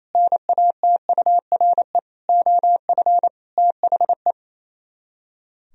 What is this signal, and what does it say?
Morse code, heard as a keyed tone, about 34 wpm, 700 Hz. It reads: NATURE OF THE
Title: Speed-Racing - Sets of 4 Words 28wpm